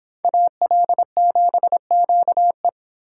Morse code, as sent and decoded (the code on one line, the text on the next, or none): .- .-.. --... --.- .
AL7QE